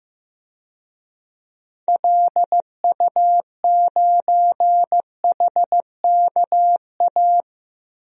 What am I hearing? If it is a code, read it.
Morse code, heard as a keyed tone, about 15 wpm, 700 Hz. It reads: LU9HKA
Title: DXCC-100 Call Signs - 6 Characters - 15wpm